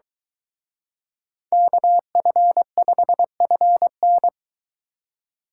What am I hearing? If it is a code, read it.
Morse code, heard as a keyed tone, about 23 wpm, 700 Hz. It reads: KF5FN